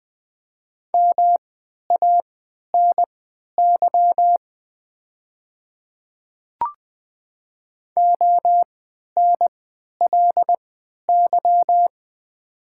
Morse code, heard as a keyed tone, about 20 wpm, spaced as for 11 wpm, 700 Hz. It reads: MANY ONLY
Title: 3x Character Spacing - Top 200 Words - 20wpm